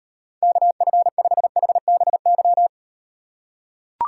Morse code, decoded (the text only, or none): KF5HBY